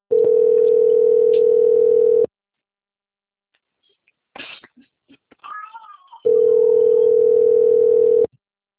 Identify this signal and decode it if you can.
Ringback tone (North America), 440 + 480 Hz.